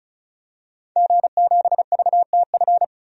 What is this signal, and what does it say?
Morse code, heard as a keyed tone, about 36 wpm, 700 Hz. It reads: G7VTF
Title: DXCC-100 Call Signs - 5 Characters - 35wpm